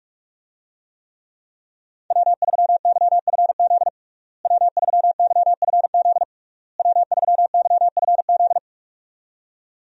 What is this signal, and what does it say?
Morse code, heard as a keyed tone, about 46 wpm, 700 Hz. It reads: W3YFZ W3YFZ W3YFZ
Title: USA Call Signs - 5 Characters - Repeated 3x - 45wpm